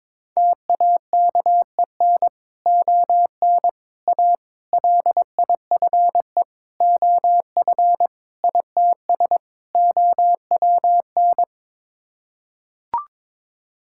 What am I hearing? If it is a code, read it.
Morse code, heard as a keyed tone, about 21 wpm, 700 Hz. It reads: TAKEN ON A LIFE OF ITS OWN
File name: practice / 22wpm / Sets of 7 Words - 22wpm